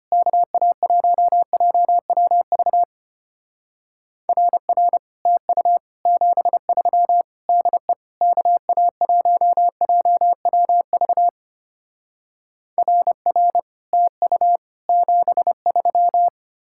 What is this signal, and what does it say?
Morse code, heard as a keyed tone, about 32 wpm, 700 Hz. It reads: KA1JWV RR TU 73 DE KA1JWV RR TU 73